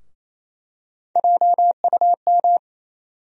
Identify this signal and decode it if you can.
Morse code, heard as a keyed tone, about 28 wpm, 700 Hz. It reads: JUM
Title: USA Call Sign Postfixes - 3 Letters 28wpm